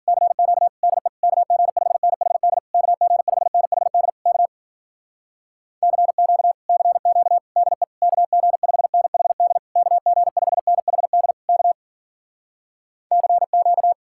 Morse code, heard as a keyed tone, about 53 wpm, 700 Hz. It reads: CQ DE KG5NHD KG5NHD K CQ CQ DE KG5NHD KG5NHD K CQ